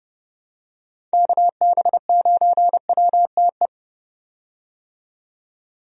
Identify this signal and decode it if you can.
Morse code, heard as a keyed tone, about 30 wpm, 700 Hz. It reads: KB9WTE
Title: USA Call Signs - 6 Characters - 30wpm